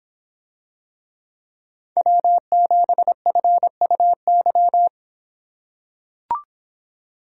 Morse code, heard as a keyed tone, about 26 wpm, 700 Hz. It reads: W7FUY